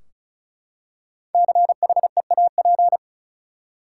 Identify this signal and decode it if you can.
Morse code, heard as a keyed tone, about 34 wpm, 700 Hz. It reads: CHEAP
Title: Five Letter Words - 35wpm